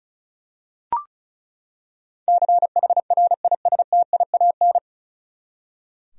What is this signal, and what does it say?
Morse code, heard as a keyed tone, about 36 wpm, 700 Hz. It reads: CHRISTIAN
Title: USA Common Names - 35wpm